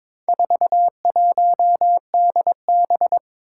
Morse code, heard as a keyed tone, about 22 wpm, 700 Hz. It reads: 41DB
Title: Four Challenging Letter-Number Combinations 22wpm